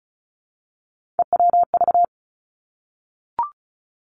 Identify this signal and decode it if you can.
Morse code, heard as a keyed tone, about 36 wpm, 700 Hz. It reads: EWV
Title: Three Challenging Letter-Number Combinations 35wpm